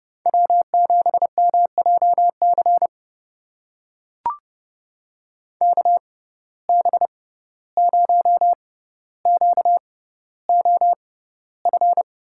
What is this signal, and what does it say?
Morse code, heard as a keyed tone, about 32 wpm, spaced as for 15 wpm, 700 Hz. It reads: W7MJC KB0QOF
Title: USA Call Signs - 6x Character Spacing - 30wpm